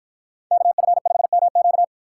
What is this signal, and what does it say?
Morse code, heard as a keyed tone, about 51 wpm, 700 Hz. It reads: KF5MQ